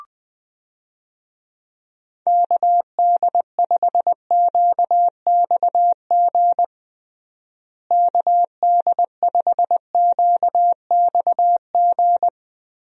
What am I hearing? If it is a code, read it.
Morse code, heard as a keyed tone, about 20 wpm, 700 Hz. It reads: KD5QXG KD5QXG
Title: USA Call Signs - 6 Characters - Repeated 3x - 20wpm